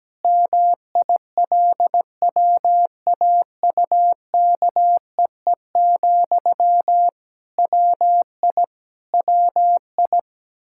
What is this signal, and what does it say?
Morse code, heard as a keyed tone, about 17 wpm, 700 Hz. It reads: MILWAUKEE, WI WI
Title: K1USN SST QSO Practice 17wpm